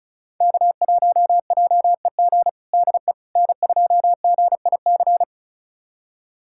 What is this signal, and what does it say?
Morse code, heard as a keyed tone, about 34 wpm, 700 Hz. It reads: K1JEG DE N2GIC
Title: USA Callsign DE Callsign 35wpm